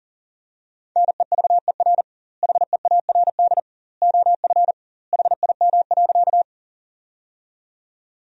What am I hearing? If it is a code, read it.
Morse code, heard as a keyed tone, about 37 wpm, 700 Hz. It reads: NEVER HEARD OF HIM.
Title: 2x Word-Spacing - Sentences from Top 400 Words - 40wpm